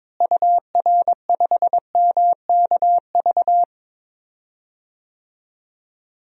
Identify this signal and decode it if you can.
Morse code, heard as a keyed tone, about 22 wpm, 700 Hz. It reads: UR5MKV
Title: DXCC-100 Call Signs - 6 Characters - Repeated 2x - 22wpm